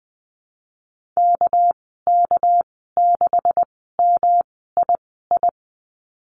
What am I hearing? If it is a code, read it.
Morse code, heard as a keyed tone, about 20 wpm, spaced as for 15 wpm, 700 Hz. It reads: KK6MII
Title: 207 Lesson 33g - Letter j - Call Signs 20wpm